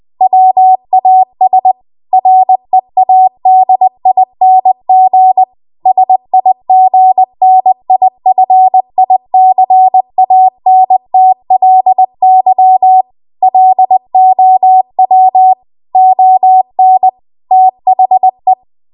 Morse code, heard as a keyed tone, about 20 wpm, 750 Hz. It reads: WAS READING SIGNIFICANTLY LOW ON THE
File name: Archive / 20 WPM / 20WPM